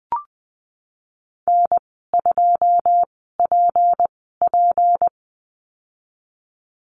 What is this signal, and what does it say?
Morse code, heard as a keyed tone, about 20 wpm, spaced as for 15 wpm, 700 Hz. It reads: N2PP
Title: 193 Lesson 31g - Letter v - Call Signs 20wpm